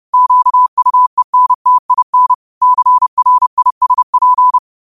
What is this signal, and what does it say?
Morse code, heard as a keyed tone, about 30 wpm, 1000 Hz. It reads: QUENTIN CRISP
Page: New quotes every day in morse code at 30 Words per minute.